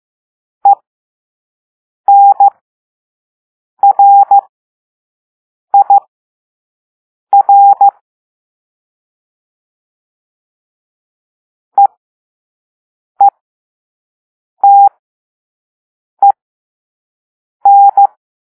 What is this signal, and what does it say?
Morse code, heard as a keyed tone, about 15 wpm, spaced as for 5 wpm, 800 Hz. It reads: ENRIR EETEN
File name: Morse / Level2-All-A